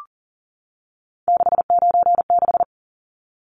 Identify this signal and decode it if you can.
Morse code, heard as a keyed tone, about 40 wpm, 700 Hz. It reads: 696
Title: Numbers - 3-Digit 40wpm